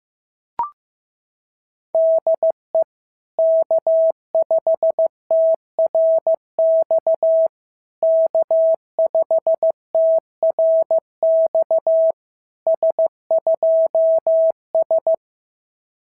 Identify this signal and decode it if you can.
Morse code, heard as a keyed tone, about 15 wpm, 650 Hz. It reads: DE K5TRX K5TRX S2S
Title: SOTA QSO Practice 15wpm